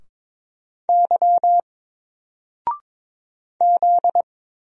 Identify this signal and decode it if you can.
Morse code, heard as a keyed tone, about 22 wpm, 700 Hz. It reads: Y Z
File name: Single-Letter-Number-22wpm